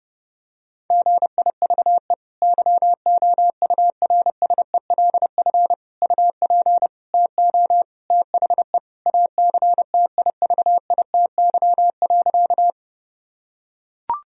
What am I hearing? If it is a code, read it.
Morse code, heard as a keyed tone, about 32 wpm, 700 Hz. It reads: GIVE YOURSELF UP TO THE ACTIVITY.
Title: Sentences from Top 1000 Words - Review Part 4 - 30wpm